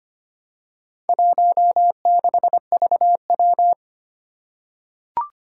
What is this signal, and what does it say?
Morse code, heard as a keyed tone, about 25 wpm, 700 Hz. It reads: 16VW
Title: Four Challenging Letter-Number Combinations 25wpm